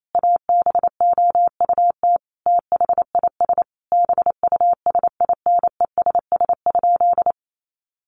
Morse code, heard as a keyed tone, about 28 wpm, 700 Hz. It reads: ABOUT THIS BUSINESS?
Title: Sentences from Top 600 Words 28wpm